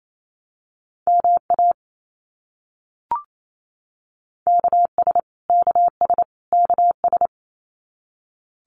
Morse code, heard as a keyed tone, about 28 wpm, 700 Hz. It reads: MA KS KS KS
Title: US States Repeated 3x 28wpm